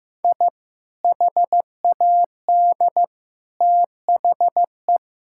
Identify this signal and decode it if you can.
Morse code, heard as a keyed tone, about 14 wpm, 700 Hz. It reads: I HAD THE
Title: Sentences from Top 1000 Words - Review Part 6 - 15wpm